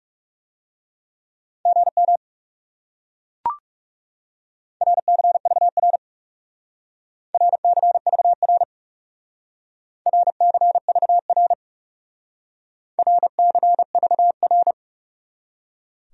Morse code, heard as a keyed tone, about 38 wpm, 700 Hz. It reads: GM RCVR RCVR RCVR RCVR